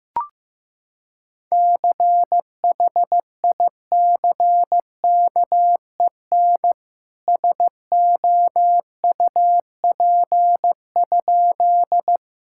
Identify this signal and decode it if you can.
Morse code, heard as a keyed tone, about 15 wpm, 700 Hz. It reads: CHICKEN SOUP?
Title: Sentences from Top 1000 Words - Review Part 6 - 15wpm